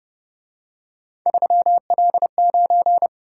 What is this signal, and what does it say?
Morse code, heard as a keyed tone, about 29 wpm, 700 Hz. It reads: 3L9
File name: Speed-Racing - Three Letter-Number Combinations - 25wpm